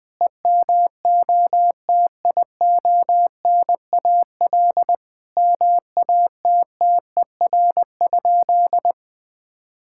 Morse code, heard as a keyed tone, about 21 wpm, 700 Hz. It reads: EMOTIONAL MATTER?
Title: Sentences from Top 1000 Words - Review Part 5 - 20wpm